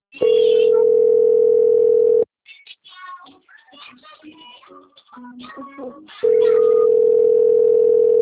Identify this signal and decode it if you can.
Ringback tone (North America), 440 + 480 Hz.
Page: Atibaia - SP